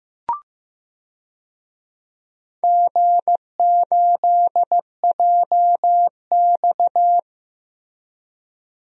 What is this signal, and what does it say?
Morse code, heard as a keyed tone, about 15 wpm, 700 Hz. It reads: G8JX
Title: DXCC-100 Call Signs - 4 Characters - Repeated 2x - 15wpm